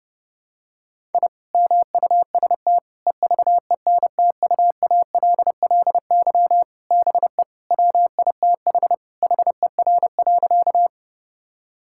Morse code, heard as a keyed tone, about 29 wpm, 700 Hz. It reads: I MUST EVENTUALLY BE WITH HER.
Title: Sentences from Top 1000 Words - Review Part 6 - 30wpm